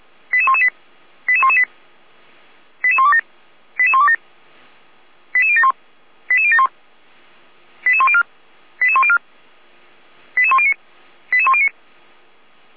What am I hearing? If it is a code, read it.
Five-tone selective call (ZVEI), heard as quick runs of five tones, tones of about 70 ms. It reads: 88118 88118 88127 88127 88071 88071 88104 88104 88109 88109